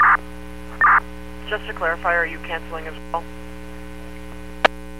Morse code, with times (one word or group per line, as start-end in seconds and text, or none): none